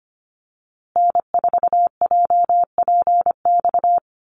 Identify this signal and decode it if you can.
Morse code, heard as a keyed tone, about 25 wpm, 700 Hz. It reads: N4JPX
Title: USA Call Signs Repeated 3x 25wpm